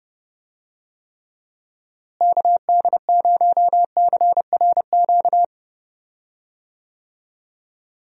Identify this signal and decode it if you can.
Morse code, heard as a keyed tone, about 30 wpm, 700 Hz. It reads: KD0CRQ
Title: USA Call Signs - 6 Characters - Repeated 3x - 30wpm